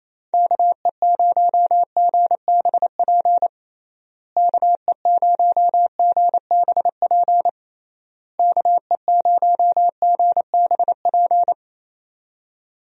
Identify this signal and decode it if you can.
Morse code, heard as a keyed tone, about 28 wpm, 700 Hz. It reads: KE0GBP KE0GBP KE0GBP
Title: USA Call Signs - 6 Characters - Repeated 3x - 28wpm